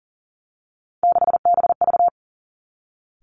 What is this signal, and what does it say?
Morse code, heard as a keyed tone, about 42 wpm, 700 Hz. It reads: 6BV
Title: Three Challenging Letter-Number Combinations 40wpm